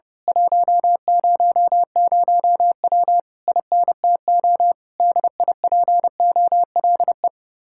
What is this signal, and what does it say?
Morse code, heard as a keyed tone, about 28 wpm, 700 Hz. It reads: 100W INTO DIPOLE
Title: Speed-Racing Common QSO Phrases - 30wpm